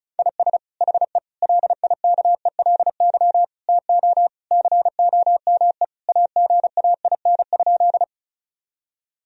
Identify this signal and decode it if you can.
Morse code, heard as a keyed tone, about 36 wpm, 700 Hz. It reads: IS HE LIKELY TO COME AGAIN?
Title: Sentences from Top 1000 Words - Review Part 4 - 35wpm